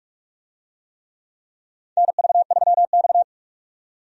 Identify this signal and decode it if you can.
Morse code, heard as a keyed tone, about 46 wpm, 700 Hz. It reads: NV3X
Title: USA Call Signs - 4 Characters - 45wpm